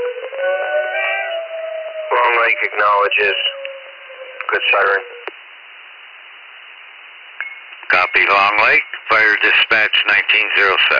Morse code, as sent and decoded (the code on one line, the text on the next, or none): none